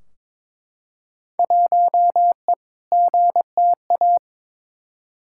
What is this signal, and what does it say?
Morse code, heard as a keyed tone, about 22 wpm, 700 Hz. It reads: 1E GTA
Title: Field Day Exchanges - 22wpm